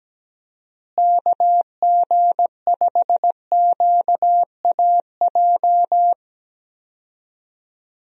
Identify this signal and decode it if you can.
Morse code, heard as a keyed tone, about 17 wpm, 700 Hz. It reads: KG5QAJ
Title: USA Call Signs - 6 Characters - Repeated 2x - 17wpm